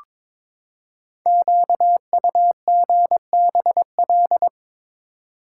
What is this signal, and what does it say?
Morse code, heard as a keyed tone, about 22 wpm, 700 Hz. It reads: QUGBL
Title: Code Groups - 5 Letters - 22wpm